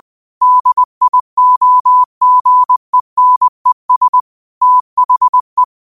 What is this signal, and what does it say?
Morse code, heard as a keyed tone, about 20 wpm, 1000 Hz. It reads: DIOGENES THE
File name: qotdfiles / 20wpm